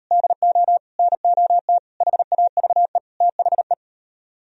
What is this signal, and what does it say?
Morse code, heard as a keyed tone, about 38 wpm, 700 Hz. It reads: DO NOT HAVE THE